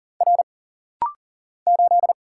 Morse code, heard as a keyed tone, about 40 wpm, 700 Hz. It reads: R 8
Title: Single Letter-Number - Rapid-Fire 40wpm